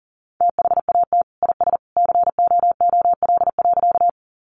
Morse code, heard as a keyed tone, about 40 wpm, 700 Hz. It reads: THAT IS COOL.
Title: Sentences from Top 1200 Words 40wpm